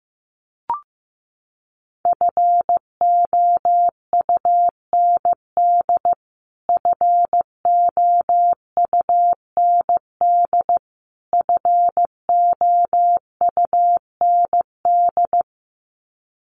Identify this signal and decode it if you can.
Morse code, heard as a keyed tone, about 15 wpm, 700 Hz. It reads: FOUND FOUND FOUND
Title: Top 300 Words Repeated 15wpm